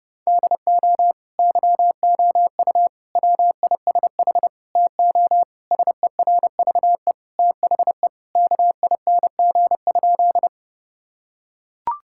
Morse code, heard as a keyed tone, about 29 wpm, 700 Hz. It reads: DO YOU WISH TO SERVE THE KING?
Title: Sentences from Top 1600 Words 30wpm